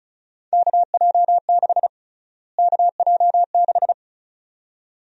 Morse code, heard as a keyed tone, about 36 wpm, 700 Hz. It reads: KJ6 KJ6
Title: USA Call Sign Prefixes - Repeated 2x - 35wpm